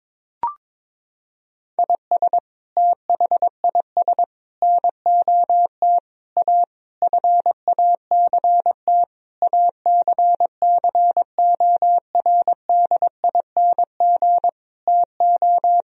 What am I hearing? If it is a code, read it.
Morse code, heard as a keyed tone, about 22 wpm, 700 Hz. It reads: IS THIS NOT A FACT ACCORDING TO
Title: Sentences from Top 1000 Words - Review Part 4 - 22wpm